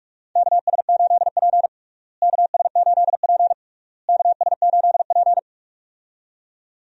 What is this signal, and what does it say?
Morse code, heard as a keyed tone, about 46 wpm, 700 Hz. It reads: KS8P KS8P KS8P
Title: USA Call Signs - 4 Characters - Repeated 3x - 45wpm